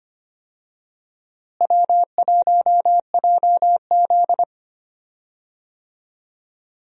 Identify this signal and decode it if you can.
Morse code, heard as a keyed tone, about 25 wpm, 700 Hz. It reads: W1JZ
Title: USA Call Signs - 4 Characters - Repeated 3x - 25wpm